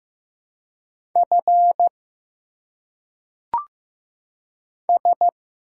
Morse code, heard as a keyed tone, about 15 wpm, 700 Hz. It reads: F S
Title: Single Letters - Phonetic Pronunciation 15wpm